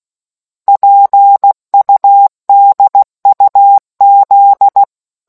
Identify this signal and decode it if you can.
Morse code, heard as a keyed tone, about 16 wpm, 800 Hz. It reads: PUDUZ